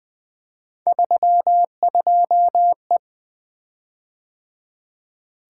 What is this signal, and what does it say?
Morse code, heard as a keyed tone, about 20 wpm, 700 Hz. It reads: 32E
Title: Three Letter-Number Combinations - 20wpm